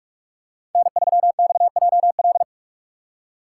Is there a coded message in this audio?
Yes: Morse code from 0.8 to 2.4 s, heard as a keyed tone, about 44 wpm, 700 Hz. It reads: N3XJL